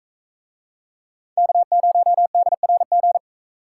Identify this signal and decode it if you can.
Morse code, heard as a keyed tone, about 44 wpm, 700 Hz. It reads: K0DRG